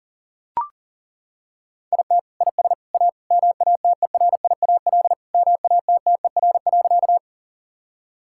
Morse code, heard as a keyed tone, about 37 wpm, 700 Hz. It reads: IT IS A MATERIAL MATTER.
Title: Sentences from Top 1000 Words - Review Part 5 - 40wpm